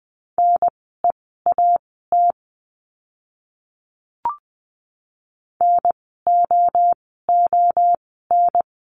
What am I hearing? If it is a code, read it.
Morse code, heard as a keyed tone, about 20 wpm, spaced as for 15 wpm, 700 Hz. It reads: NEAT NOON
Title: 054 Lesson 10f - Letter r - Words 20wpm